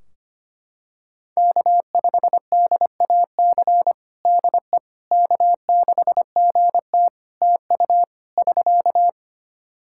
Morse code, heard as a keyed tone, about 25 wpm, 700 Hz. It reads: K5DAC DE K6GT TU <SK>
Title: Practice QSO - Exchange 4 - Ending - 25wpm